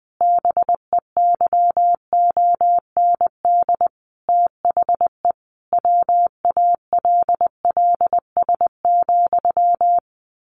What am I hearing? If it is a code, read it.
Morse code, heard as a keyed tone, about 20 wpm, 700 Hz. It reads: BEYOND THE WALLS,
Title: A Sound Of Thunder 20wpm